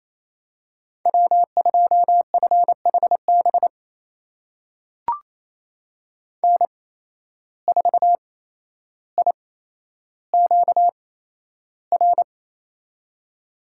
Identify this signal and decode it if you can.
Morse code, heard as a keyed tone, about 28 wpm, spaced as for 12 wpm, 700 Hz. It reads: W2FHB N4IQR
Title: USA Call Signs - 5 Characters - 8x Character Spacing - 28wpm